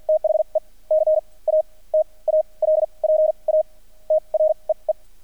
Morse code, heard as a keyed tone, about 48 wpm, spaced as for 30 wpm, 620 Hz. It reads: THE CATALPA TREE
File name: archived / CRSnet